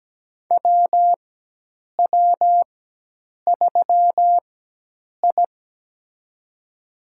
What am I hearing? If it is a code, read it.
Morse code, heard as a keyed tone, about 17 wpm, spaced as for 8 wpm, 700 Hz. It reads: WW3I
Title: USA Call Signs - 4 Characters - 4x Character Spacing - 17wpm